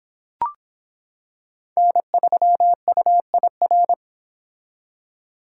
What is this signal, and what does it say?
Morse code, heard as a keyed tone, about 25 wpm, 700 Hz. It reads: N3UIR